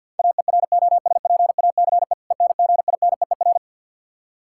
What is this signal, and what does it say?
Morse code, heard as a keyed tone, about 51 wpm, 700 Hz. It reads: AEROSPACE ENGINEER
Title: Practice QSO - Exchange 3 - More Info - 50wpm